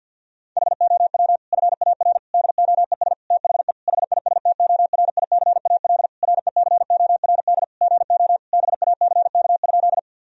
Morse code, heard as a keyed tone, about 52 wpm, 700 Hz. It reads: HOW FAR DOES THE HISTORICAL RECORD GO BACK?
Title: Sentences from Top 1000 Words - Review Part 4 - 50wpm